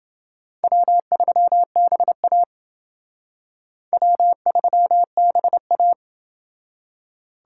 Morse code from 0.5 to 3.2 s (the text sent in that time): W3BA